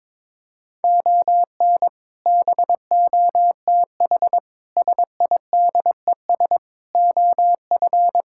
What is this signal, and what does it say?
Morse code, heard as a keyed tone, about 22 wpm, 700 Hz. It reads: ON BOTH SIDES OF
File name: practice / 22wpm / Build A Phrase - 5-Word Phrases - 22wpm